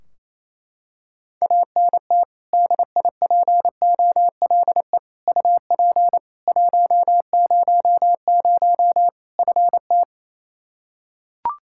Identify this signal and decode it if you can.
Morse code, heard as a keyed tone, about 28 wpm, 700 Hz. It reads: ANT DIPOLE UP 100 FT
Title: Practice QSO - Exchange 2 - Info - 28wpm